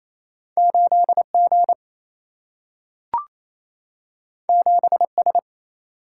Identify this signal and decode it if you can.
Morse code, heard as a keyed tone, about 28 wpm, 700 Hz. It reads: 8G 7S
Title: Single-Letter Single-Number 28wpm